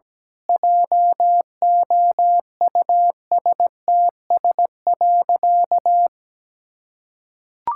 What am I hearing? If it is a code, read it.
Morse code, heard as a keyed tone, about 17 wpm, 700 Hz. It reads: JOUSTS.
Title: Pangram Challenge v2 - 17wpm